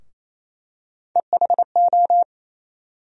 Morse code, heard as a keyed tone, about 28 wpm, 700 Hz. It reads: EHO
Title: USA Call Sign Postfixes - 3 Letters 28wpm